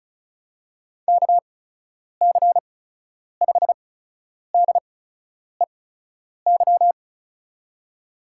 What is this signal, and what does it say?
Morse code, heard as a keyed tone, about 36 wpm, spaced as for 10 wpm, 700 Hz. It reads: KC5DEY
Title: USA Call Signs - 6 Characters - 8x Character Spacing - 35wpm